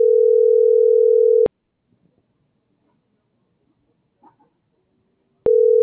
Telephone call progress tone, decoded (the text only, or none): ringback tone